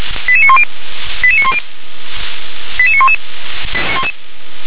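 Five-tone selective call (ZVEI), heard as quick runs of five tones, tones of about 80 ms. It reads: 88910 88910 88911
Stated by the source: ZVEI five-tone call received on SDR.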